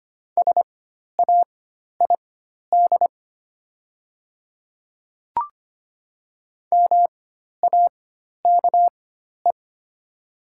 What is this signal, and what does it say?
Morse code, heard as a keyed tone, about 25 wpm, spaced as for 12 wpm, 700 Hz. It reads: SAID MAKE
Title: 4x Character Spacing - Top 100 Words - 25wpm